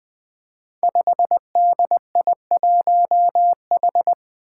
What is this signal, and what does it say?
Morse code, heard as a keyed tone, about 20 wpm, 700 Hz. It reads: 5DI1H
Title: Five Challenging Letter-Number Combinations 20wpm